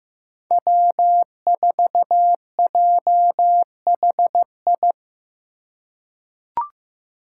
Morse code, heard as a keyed tone, about 15 wpm, 700 Hz. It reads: W4JHI
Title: Five Challenging Letter-Number Combinations 15wpm